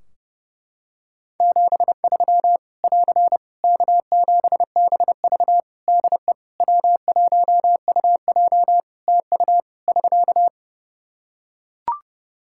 Morse code, heard as a keyed tone, about 32 wpm, 700 Hz. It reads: 73 <AR> K7BV DE W1UJ TU <SK>
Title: K1USN SST QSO Practice 30wpm